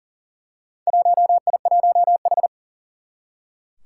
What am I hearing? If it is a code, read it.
Morse code, heard as a keyed tone, about 41 wpm, 700 Hz. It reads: 1I1H
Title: Four Challenging Letter-Number Combinations 40wpm